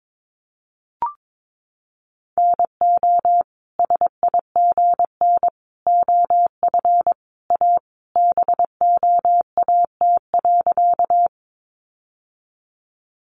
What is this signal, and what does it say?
Morse code, heard as a keyed tone, about 22 wpm, 700 Hz. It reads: NO SIGN OF A BOAT.
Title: Sentences from Top 1000 Words 22wpm